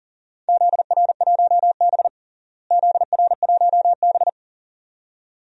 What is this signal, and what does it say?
Morse code, heard as a keyed tone, about 40 wpm, 700 Hz. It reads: ZR1B ZR1B
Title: DXCC-100 Call Signs - 4 Characters - Repeated 2x - 40wpm